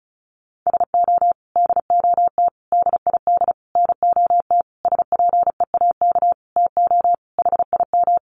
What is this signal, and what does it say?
Morse code, heard as a keyed tone, about 34 wpm, 700 Hz. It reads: SO DOT DID NOT SPEAK TO HIM
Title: Sentences from Top 1500 Words 35wpm